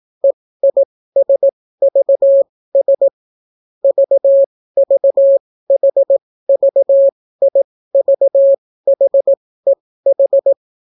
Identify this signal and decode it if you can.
Morse code, heard as a keyed tone, about 18 wpm, spaced as for 14 wpm, 550 Hz. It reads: EISVS VVHVIVHEH